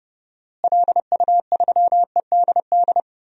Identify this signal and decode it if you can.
Morse code, heard as a keyed tone, about 30 wpm, 700 Hz. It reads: LU3EDD
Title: DXCC-100 Call Signs - 6 Characters - Repeated 2x - 30wpm